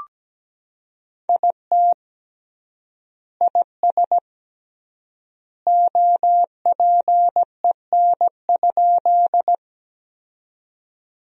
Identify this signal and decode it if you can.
Morse code, heard as a keyed tone, about 17 wpm, 700 Hz. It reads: IT IS OPEN?
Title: 3x Word-Spacing - Sentences from Top 300 Words - 17wpm